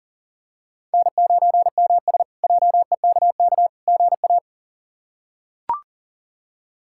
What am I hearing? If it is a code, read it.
Morse code, heard as a keyed tone, about 40 wpm, 700 Hz. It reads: N9MS JEKK GA
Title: K1USN SST QSO Practice - 40wpm